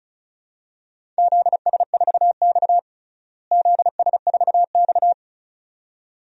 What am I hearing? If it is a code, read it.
Morse code, heard as a keyed tone, about 34 wpm, 700 Hz. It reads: ZS4X ZS4X
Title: DXCC-100 Call Signs - 4 Characters - Repeated 2x - 35wpm